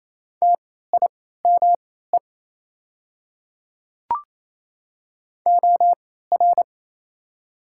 Morse code, heard as a keyed tone, about 28 wpm, spaced as for 16 wpm, 700 Hz. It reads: TIME OR
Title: 3x Character Spacing - Top 100 Words - 28wpm